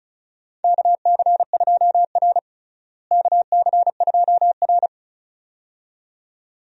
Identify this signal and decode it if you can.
Morse code, heard as a keyed tone, about 36 wpm, 700 Hz. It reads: KC2R KC2R
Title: USA Call Signs - 4 Characters - Repeated 2x - 35wpm